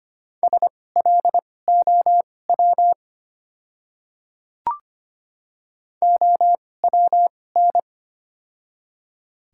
Morse code, heard as a keyed tone, about 25 wpm, spaced as for 18 wpm, 700 Hz. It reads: SLOW OWN
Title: 2x Character Spacing - Top 500 Words - 25wpm